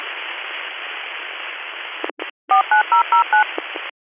DTMF, heard as telephone touch tones, tones of about 120 ms. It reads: 49009
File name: 101- .0-call 2782